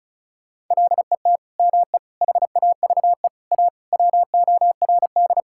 Morse code, heard as a keyed tone, about 34 wpm, 700 Hz. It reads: LET ME HAVE A WORD
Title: Sentences from Top 1000 Words - Review Part 6 - 35wpm